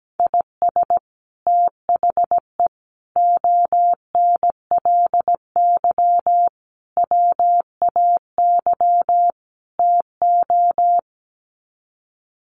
Morse code, heard as a keyed tone, about 17 wpm, 700 Hz. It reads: IS THE ONLY WAY TO
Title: Sets of 5 Words Encore 17wpm